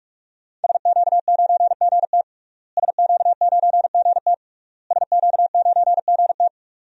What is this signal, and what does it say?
Morse code, heard as a keyed tone, about 41 wpm, 700 Hz. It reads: SQ9GT SQ9GT SQ9GT
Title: DXCC-100 Call Signs - 5 Characters - Repeated 3x - 45wpm